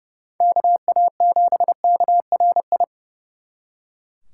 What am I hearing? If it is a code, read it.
Morse code, heard as a keyed tone, about 35 wpm, 700 Hz. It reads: KA7KRI
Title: Speed Racing - USA Call Signs 30wpm